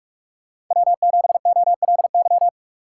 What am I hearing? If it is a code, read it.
Morse code, heard as a keyed tone, about 45 wpm, 700 Hz. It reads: W7OLY
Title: USA Call Signs - 5 Characters - 3x Character Spacing - 45wpm